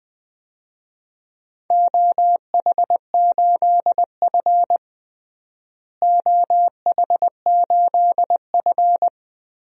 Morse code, heard as a keyed tone, about 20 wpm, 700 Hz. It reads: OH8F OH8F
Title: DXCC-100 Call Signs - 4 Characters - Repeated 3x - 20wpm